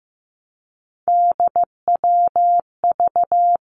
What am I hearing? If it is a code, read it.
Morse code, heard as a keyed tone, about 15 wpm, 700 Hz. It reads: DWV
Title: Three Challenging Letter-Number Combinations 15wpm